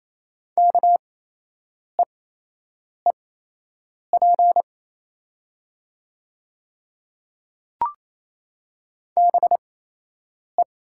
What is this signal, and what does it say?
Morse code, heard as a keyed tone, about 28 wpm, spaced as for 8 wpm, 690 Hz. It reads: KEEP BE
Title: 8x Character Spacing - Top 300 Words - 28wpm